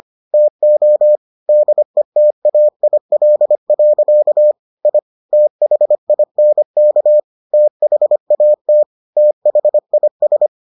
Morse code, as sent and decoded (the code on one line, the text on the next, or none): - --- / -.. - .- .. .-.. .-.-.- / .. / - .... .. -. -.- / - .... .- - / - .... .. ...
TO DETAIL. I THINK THAT THIS